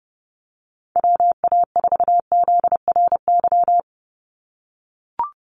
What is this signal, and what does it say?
Morse code, heard as a keyed tone, about 30 wpm, 700 Hz. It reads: WA4ZRY
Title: USA Call Signs 30wpm